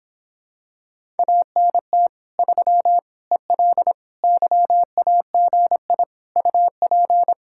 Morse code, heard as a keyed tone, about 26 wpm, 700 Hz. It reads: ANT 3 EL YAGI UP